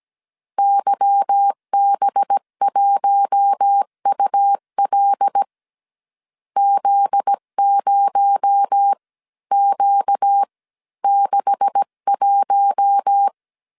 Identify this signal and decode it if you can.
Morse code, heard as a keyed tone, about 17 wpm, spaced as for 14 wpm, 800 Hz. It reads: YB1UL Z0Q61
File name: CW / DET / 1 9 1500C S B 16wpm 2856